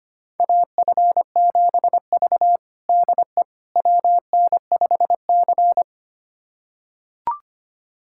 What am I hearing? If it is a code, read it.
Morse code, heard as a keyed tone, about 25 wpm, 700 Hz. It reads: AF7V DE WN5C
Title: USA Callsign DE Callsign - 4 Characters - 25wpm